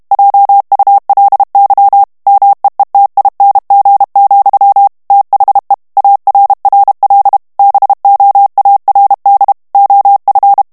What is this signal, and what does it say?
Morse code, heard as a keyed tone, about 32 wpm, 800 Hz. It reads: JULY MEETING, THE ARRL BOARD OF
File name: ANGLAIS texte clair 3 vitesse 30 mots minute